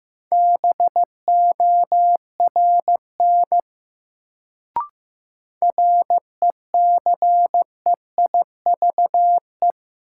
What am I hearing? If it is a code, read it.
Morse code, heard as a keyed tone, about 15 wpm, 700 Hz. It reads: BORN RECEIVE
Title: Sentences from Top 1000 Words - New Word Drill - 15wpm